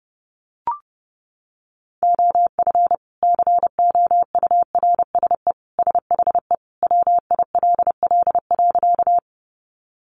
Morse code, heard as a keyed tone, about 30 wpm, 700 Hz. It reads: OF COURSE SHE WILL.
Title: Sentences from Top 500 Words 30wpm